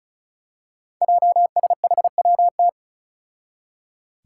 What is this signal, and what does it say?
Morse code, heard as a keyed tone, about 36 wpm, 700 Hz. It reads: JSHWT
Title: Five Challenging Letter-Number Combinations 35wpm